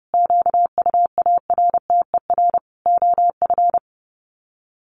Morse code, heard as a keyed tone, about 30 wpm, 700 Hz. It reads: QUARTER OF
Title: Sets of 5 Words Encore2 30wpm